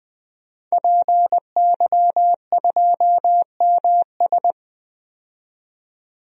Morse code, heard as a keyed tone, about 20 wpm, 700 Hz. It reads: PY2MS
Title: DXCC-100 Call Signs - 5 Characters - 20wpm